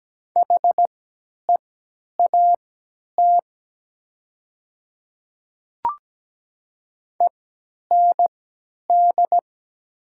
Morse code, heard as a keyed tone, about 17 wpm, spaced as for 10 wpm, 700 Hz. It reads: HEAT END
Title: 3x Character Spacing - Top 500 Words - 17wpm